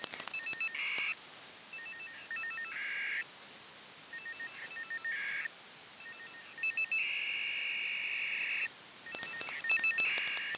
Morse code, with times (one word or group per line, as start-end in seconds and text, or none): none